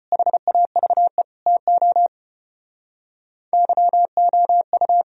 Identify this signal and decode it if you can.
Morse code, heard as a keyed tone, about 33 wpm, 700 Hz. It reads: HAVE TO YOU